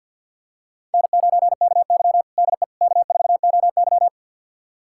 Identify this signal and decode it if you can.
Morse code, heard as a keyed tone, about 52 wpm, 700 Hz. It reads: N9KY DE K4OY
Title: USA Callsign DE Callsign - 4 Characters - 50wpm